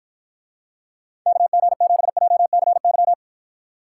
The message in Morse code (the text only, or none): KG7JCY